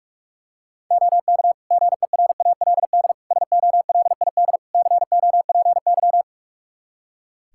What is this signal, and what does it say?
Morse code, heard as a keyed tone, about 42 wpm, 700 Hz. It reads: OK GERALD SOLID COPY
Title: Practice QSO - Exchange 3 - More Info - 45wpm